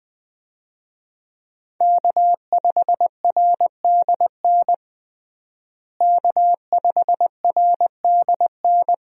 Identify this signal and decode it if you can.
Morse code, heard as a keyed tone, about 21 wpm, 700 Hz. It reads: K5RDN K5RDN
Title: USA Call Signs - 5 Characters - Repeated 2x - 20wpm